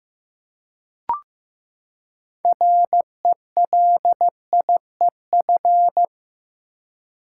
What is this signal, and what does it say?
Morse code, heard as a keyed tone, about 15 wpm, 700 Hz. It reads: RELIEF
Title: Six Letter Words - 15wpm